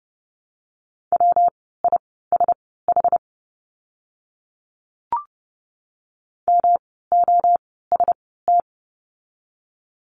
Morse code, heard as a keyed tone, about 30 wpm, spaced as for 17 wpm, 700 Hz. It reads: WISH MOST
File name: course / 30:10wpm / 115 Lesson 19f - Letter w - Words 30wpm